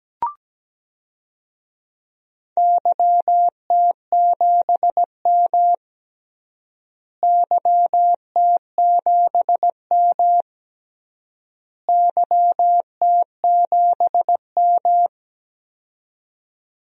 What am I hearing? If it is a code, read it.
Morse code, heard as a keyed tone, about 17 wpm, 700 Hz. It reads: YT7M YT7M YT7M
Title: DXCC-100 Call Signs - 4 Characters - Repeated 3x - 17wpm